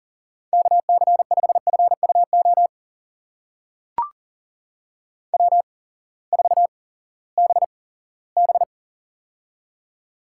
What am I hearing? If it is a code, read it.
Morse code, heard as a keyed tone, about 40 wpm, spaced as for 20 wpm, 700 Hz. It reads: KC5FUO W4BB
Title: USA Call Signs - 8x Character Spacing - 40wpm